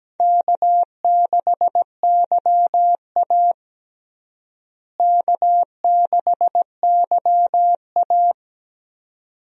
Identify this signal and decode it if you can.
Morse code, heard as a keyed tone, about 17 wpm, 700 Hz. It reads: K6YA K6YA
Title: USA Call Signs - 4 Characters - Repeated 2x - 17wpm